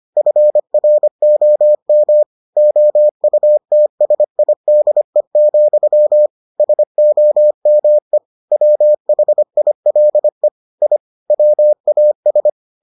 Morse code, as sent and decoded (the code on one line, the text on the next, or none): ..-. .-. --- -- / --- ..- - ... .. -.. --..-- / ... --- -- . / .-- .... .. .-.. / .. / .-- .- ...
FROM OUTSIDE, SOME WHILE I WAS